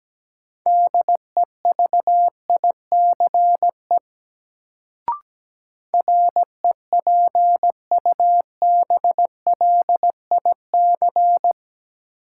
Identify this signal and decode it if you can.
Morse code, heard as a keyed tone, about 17 wpm, 700 Hz. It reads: DEVICE REPUBLIC
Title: Sentences from Top 1500 Words - New Word Drill - 17wpm